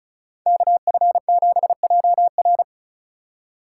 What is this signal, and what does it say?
Morse code, heard as a keyed tone, about 36 wpm, 700 Hz. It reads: KF7JR
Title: USA Call Signs - 5 Characters - 4x Character Spacing - 35wpm